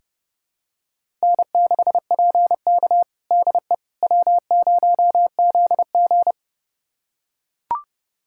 Morse code, heard as a keyed tone, about 30 wpm, 700 Hz. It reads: N6PK DE W0ZG
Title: USA Callsign DE Callsign - 4 Characters - 30wpm